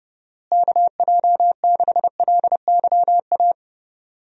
Morse code, heard as a keyed tone, about 32 wpm, 700 Hz. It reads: KJ6LYA
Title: USA Call Signs - 6 Characters - Repeated 2x - 30wpm